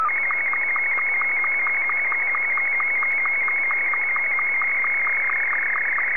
SSTV (slow-scan television), Martin 2.
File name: sstv / M2-VK3DFL